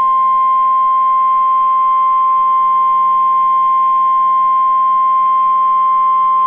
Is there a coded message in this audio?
Yes: Emergency Alert System (SAME) tones (the 1050 Hz attention tone) from 0.4 to 5.6 s.